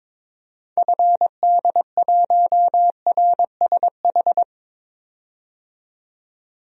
Morse code, heard as a keyed tone, about 22 wpm, 700 Hz. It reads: FD1RSH
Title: DXCC-100 Call Signs - 6 Characters - 22wpm